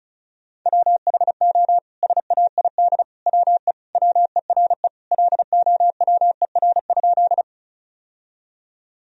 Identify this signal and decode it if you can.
Morse code, heard as a keyed tone, about 36 wpm, 700 Hz. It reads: WHO SAID WE WERE LOWER?
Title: Sentences from Top 1000 Words - Review Part 4 - 35wpm